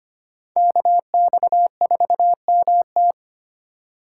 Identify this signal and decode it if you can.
Morse code, heard as a keyed tone, about 32 wpm, 700 Hz. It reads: KX4MT